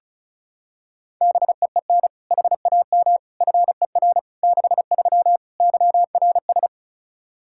Morse code, heard as a keyed tone, about 36 wpm, 700 Hz. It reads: BEEN HAM FER 63 YRS
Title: Practice QSO - Exchange 3 - More Info - 35wpm